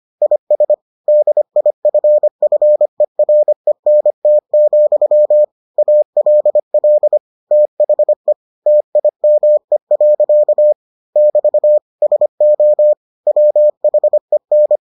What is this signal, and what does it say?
Morse code, heard as a keyed tone, about 25 wpm, 600 Hz. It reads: IS DIFFERENT, ALL THE TIME. <BT> SO WHEN